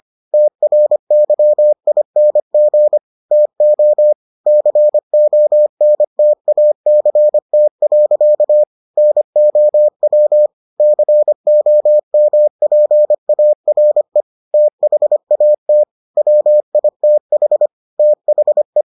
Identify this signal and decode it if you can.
Morse code, heard as a keyed tone, about 25 wpm, 600 Hz. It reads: TRYING TO CONTACT. NOW COMPARE THAT WITH THE